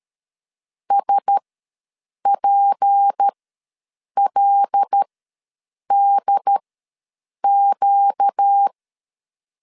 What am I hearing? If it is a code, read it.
Morse code, heard as a keyed tone, about 13 wpm, spaced as for 7 wpm, 800 Hz. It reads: SPLDQ
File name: LEZIONI / 6 PKX0 / 6 PK ws 12wpm 3116